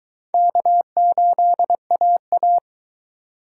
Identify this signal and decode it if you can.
Morse code, heard as a keyed tone, about 22 wpm, 700 Hz. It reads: K8AA